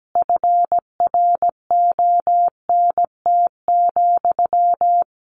Morse code, heard as a keyed tone, about 17 wpm, 700 Hz. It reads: FRONT,
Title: A Sound Of Thunder 17wpm